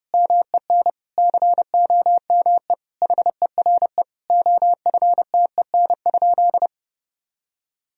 Morse code, heard as a keyed tone, about 30 wpm, 700 Hz. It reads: MEN COME HERE OFTEN?
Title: Sentences from Top 1000 Words - Review Part 4 - 30wpm